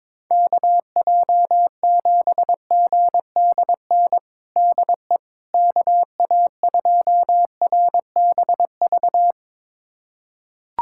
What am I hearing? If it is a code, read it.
Morse code, heard as a keyed tone, about 22 wpm, 700 Hz. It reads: KJ7GDN DE KA2RBV
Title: USA Callsign DE Callsign - 6 Characters - 22wpm